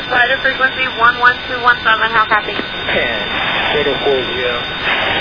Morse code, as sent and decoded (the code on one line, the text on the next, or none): none